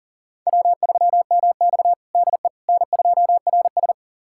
Morse code, heard as a keyed tone, about 40 wpm, 700 Hz. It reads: W3MX DE N2RS
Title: USA Callsign DE Callsign - 4 Characters - 40wpm